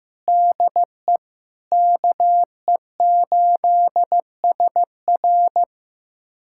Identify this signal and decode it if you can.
Morse code, heard as a keyed tone, about 15 wpm, 700 Hz. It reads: DE KE8SR
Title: USA Callsign DE Callsign - 5 Characters - 15wpm